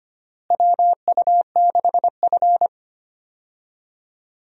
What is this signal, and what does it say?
Morse code, heard as a keyed tone, about 25 wpm, 700 Hz. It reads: WU6F
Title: USA Call Signs - 4 Characters - Repeated 3x - 25wpm